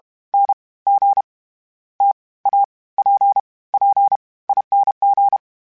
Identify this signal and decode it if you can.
Morse code, heard as a keyed tone, about 32 wpm, 800 Hz. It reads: N G T A P P I N G
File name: Variable Speed / Variable Speed-mp3s / T AOEI's sp @30